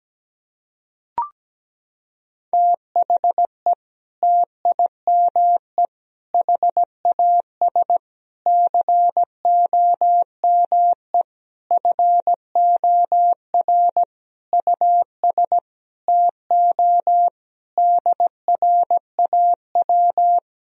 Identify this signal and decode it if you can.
Morse code, heard as a keyed tone, about 17 wpm, 700 Hz. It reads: THE TIME HAS COME FOR US TO DRAW
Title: Sentences from Top 1000 Words - Review Part 1 - 17wpm